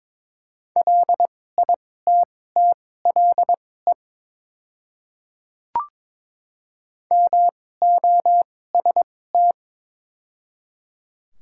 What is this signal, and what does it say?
Morse code, heard as a keyed tone, about 22 wpm, spaced as for 16 wpm, 700 Hz. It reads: LITTLE MOST
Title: 2x Character Spacing - Top 300 Words - 22wpm